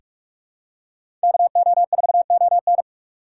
Morse code, heard as a keyed tone, about 45 wpm, 690 Hz. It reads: KO4ON
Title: USA Call Signs - 5 Characters - 2x Character Spacing - 45wpm